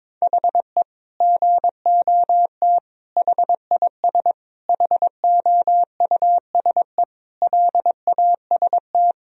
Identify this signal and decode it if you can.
Morse code, heard as a keyed tone, about 21 wpm, 700 Hz. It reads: HE GOT HIS HOUSE LAST
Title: Sentences from Top 1000 Words - Review Part 1 - 22wpm